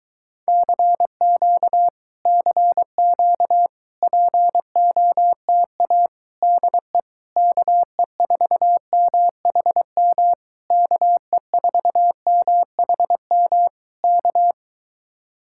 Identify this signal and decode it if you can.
Morse code, heard as a keyed tone, about 22 wpm, 700 Hz. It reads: CQ CQ POTA DE KE4MHM KE4MHM K